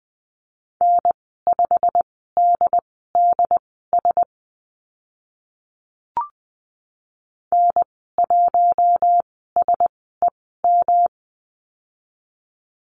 Morse code, heard as a keyed tone, about 20 wpm, spaced as for 14 wpm, 700 Hz. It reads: N5DDS N1SEM
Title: 122 Lesson 20g - Number 3 - Call Signs 20wpm